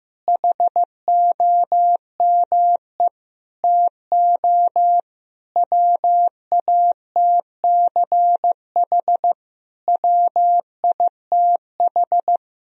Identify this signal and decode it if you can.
Morse code, heard as a keyed tone, about 15 wpm, 700 Hz. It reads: HOME TO WATCH WITH
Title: Sentences from Top 1000 Words - Review Part 4 - 15wpm